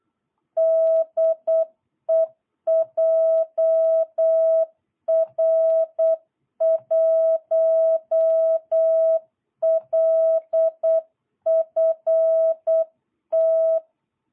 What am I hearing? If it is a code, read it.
Morse code, heard as a keyed tone, about 8 wpm, 660 Hz. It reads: DE JR1LFT
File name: jh8cbh / 2022cw